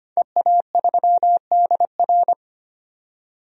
Morse code, heard as a keyed tone, about 25 wpm, 700 Hz. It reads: EA3DR
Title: DXCC-100 Call Signs - 5 Characters - 25wpm